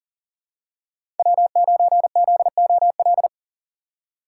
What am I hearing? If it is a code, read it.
Morse code, heard as a keyed tone, about 40 wpm, 700 Hz. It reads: W9ZOL
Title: USA Call Signs - 5 Characters - 3x Character Spacing - 40wpm